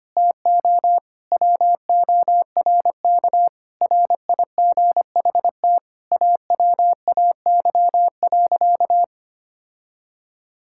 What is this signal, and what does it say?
Morse code, heard as a keyed tone, about 25 wpm, 700 Hz. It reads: TO WORK RIGHT AWAY.
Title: Sentences from Top 1600 Words 25wpm